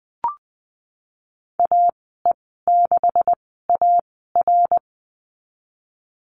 Morse code, heard as a keyed tone, about 20 wpm, spaced as for 15 wpm, 700 Hz. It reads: AE6AR
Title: 128 Lesson 21g - Number 6 - Call Signs 20wpm